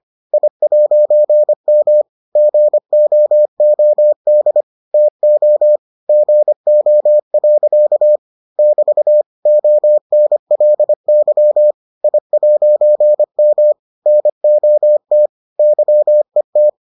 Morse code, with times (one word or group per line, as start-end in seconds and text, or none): 0.3-2.0 I'M
2.4-4.6 GOOD
4.9-5.8 TO
6.1-8.2 GO.
8.6-9.2 <BT>
9.5-11.7 ONLY
12.0-13.7 I'M
14.1-15.3 NOT
15.6-16.7 YET